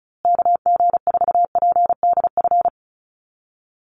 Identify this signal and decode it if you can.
Morse code, heard as a keyed tone, about 34 wpm, 700 Hz. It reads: KG4PDF
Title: USA Call Signs Repeated 3x 35wpm